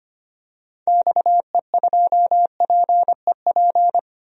Morse code, heard as a keyed tone, about 25 wpm, 700 Hz. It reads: XE2PEP
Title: DXCC-100 Call Signs - 6 Characters - Repeated 2x - 25wpm